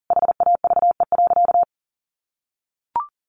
Morse code, heard as a keyed tone, about 40 wpm, 700 Hz. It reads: HAVE.
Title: Sentences from Top 1100 Words 40wpm